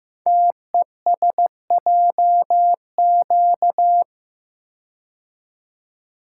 Morse code, heard as a keyed tone, about 15 wpm, 700 Hz. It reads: TESJQ
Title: Code Groups - 5 Letters - 15wpm